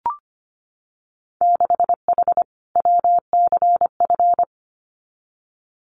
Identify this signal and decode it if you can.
Morse code, heard as a keyed tone, about 25 wpm, 700 Hz. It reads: 6H WCF
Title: Winter Field Day Exchanges Encore 25wpm